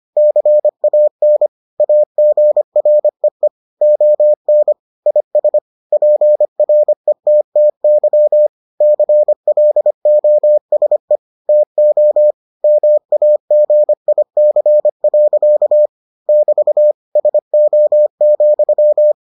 Morse code, heard as a keyed tone, about 25 wpm, 600 Hz. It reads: CAN AGREE ON IS PRETTY CLOSE TO MAGIC. <BT> SO,